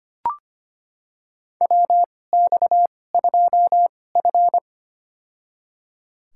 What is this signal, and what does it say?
Morse code, heard as a keyed tone, about 25 wpm, spaced as for 18 wpm, 700 Hz. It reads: WX2F
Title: USA Call Signs - 2x Character Spacing - 25wpm